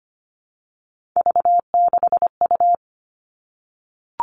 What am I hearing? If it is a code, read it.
Morse code, heard as a keyed tone, about 25 wpm, 700 Hz. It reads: V6U
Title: Three Challenging Letter-Number Combinations 25wpm